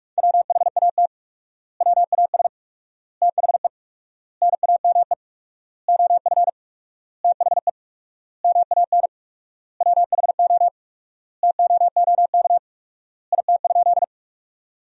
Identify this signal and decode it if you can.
Morse code, heard as a keyed tone, about 46 wpm, 700 Hz. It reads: WHAT WAS THE NAME OF THE MAN WHO TOOK IT?
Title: 4x Word-Spacing - Sentences from Top 300 Words - 45wpm